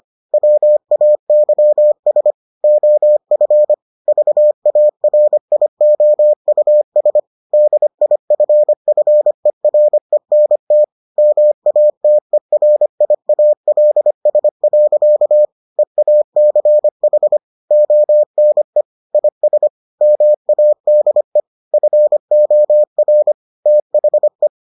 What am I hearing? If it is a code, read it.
Morse code, heard as a keyed tone, about 25 wpm, 600 Hz. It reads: WAYS OF VARIOUS DIFFERENT MATERIALS. EACH ONE IS MADE FOR THE